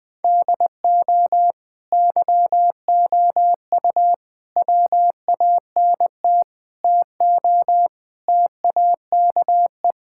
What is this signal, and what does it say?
Morse code, heard as a keyed tone, about 20 wpm, 700 Hz. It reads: DO YOU WANT TO TAKE
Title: Sentences with 3-Word Phrases - 20wpm